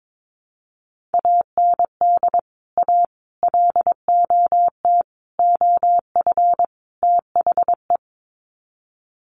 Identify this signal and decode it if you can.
Morse code, heard as a keyed tone, about 22 wpm, 700 Hz. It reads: AND A LOT OF THE
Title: Sets of 5 Words Encore 22wpm